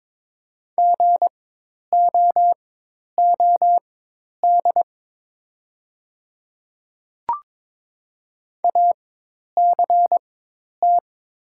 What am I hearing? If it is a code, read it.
Morse code, heard as a keyed tone, about 22 wpm, spaced as for 10 wpm, 700 Hz. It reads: GOOD ACT
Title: 4x Character Spacing - Top 200 Words - 22wpm